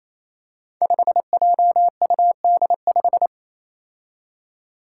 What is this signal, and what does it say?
Morse code, heard as a keyed tone, about 28 wpm, 700 Hz. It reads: 5JUD5
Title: Five Challenging Letter-Number Combinations 28wpm